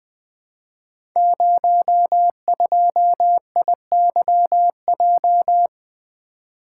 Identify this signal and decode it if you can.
Morse code, heard as a keyed tone, about 20 wpm, 700 Hz. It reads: 02IYJ
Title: Code Groups - 5 Letter:Numbers - 20wpm